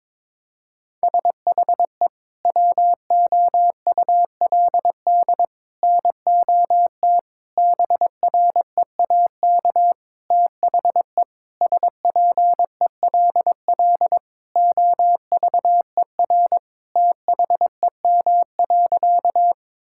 Morse code, heard as a keyed tone, about 22 wpm, 700 Hz. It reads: SHE WOULD NOT BREAK THE SPELL OVER THEM.
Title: Sentences from Top 1000 Words - Review Part 6 - 22wpm